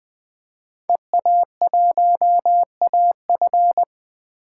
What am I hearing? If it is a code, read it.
Morse code, heard as a keyed tone, about 20 wpm, 700 Hz. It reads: EA1AF
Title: 2023 POTA Hunter Call Signs - 20wpm